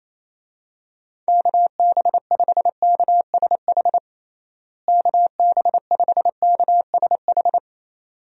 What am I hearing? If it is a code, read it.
Morse code, heard as a keyed tone, about 28 wpm, 700 Hz. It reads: KB5KSH KB5KSH
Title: USA Call Signs - 6 Characters - Repeated 2x - 28wpm